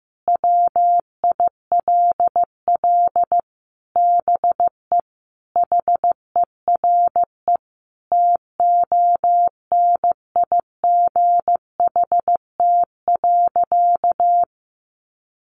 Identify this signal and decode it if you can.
Morse code, heard as a keyed tone, about 15 wpm, 700 Hz. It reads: WILL BE HERE TONIGHT.
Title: Sentences from Top 600 Words 15wpm